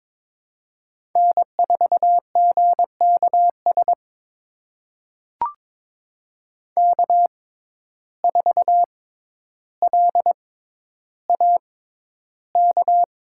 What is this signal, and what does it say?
Morse code, heard as a keyed tone, about 22 wpm, spaced as for 11 wpm, 700 Hz. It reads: N4GKS K4LAK
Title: USA Call Signs - 6x Character Spacing - 22wpm